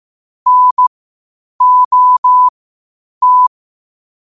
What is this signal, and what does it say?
Morse code, heard as a keyed tone, about 15 wpm, spaced as for 8 wpm, 1000 Hz. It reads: NOT
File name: qotdfiles / 8wpm